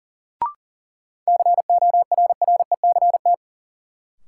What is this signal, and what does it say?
Morse code, heard as a keyed tone, about 40 wpm, 700 Hz. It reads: CORRECT
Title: Sentences from Top 700 Words - New Word Drill - 40wpm